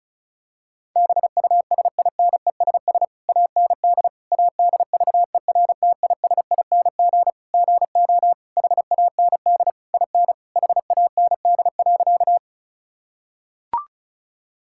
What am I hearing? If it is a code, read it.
Morse code, heard as a keyed tone, about 36 wpm, 690 Hz. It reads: BUSINESS AND ADVERTISING GO HAND IN HAND.
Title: Sentences from Top 1000 Words - Review Part 6 - 35wpm